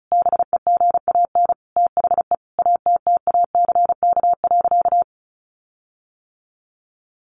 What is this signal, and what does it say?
Morse code, heard as a keyed tone, about 34 wpm, 700 Hz. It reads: BEGAN THE ATTACK.
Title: Sentences from Top 1500 Words 35wpm